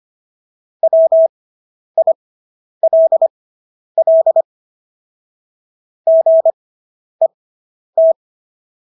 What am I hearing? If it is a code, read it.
Morse code, heard as a keyed tone, about 25 wpm, spaced as for 10 wpm, 650 Hz. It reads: WILL GET